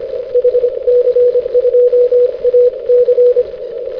Morse code, heard as a keyed tone, about 25 wpm, 490 Hz. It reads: HC2AC
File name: hc2ac- -x